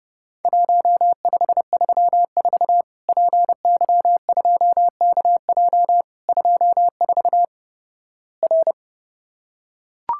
Morse code, heard as a keyed tone, about 30 wpm, 700 Hz. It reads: 1534 PY2KJ 24 R
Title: Worked All Europe DX Contest - QTC Exchange - 30wpm